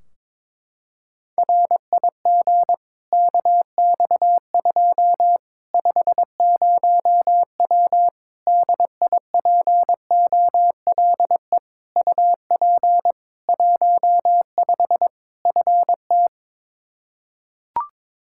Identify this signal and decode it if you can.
Morse code, heard as a keyed tone, about 22 wpm, 700 Hz. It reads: RIG KX2 50W DIPOLE UP 15 FT
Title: K1USN SST QSO Practice 22wpm